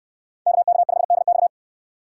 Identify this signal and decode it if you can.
Morse code, heard as a keyed tone, about 69 wpm, 700 Hz. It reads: KK5DX